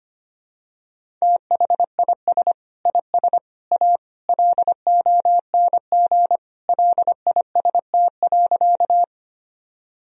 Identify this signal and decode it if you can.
Morse code, heard as a keyed tone, about 25 wpm, 700 Hz. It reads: THIS IS A LONG LIST.
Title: Sentences from Top 1000 Words - Review Part 2 - 25wpm